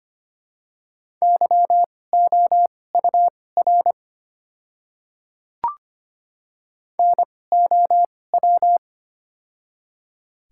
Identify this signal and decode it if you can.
Morse code, heard as a keyed tone, about 25 wpm, spaced as for 18 wpm, 700 Hz. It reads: YOUR NOW
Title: 2x Character Spacing - Top 300 Words - 25wpm